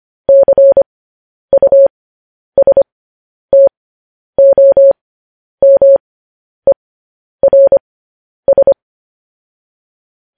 Morse code, heard as a keyed tone, about 25 wpm, spaced as for 10 wpm, 550 Hz. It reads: CUSTOMERS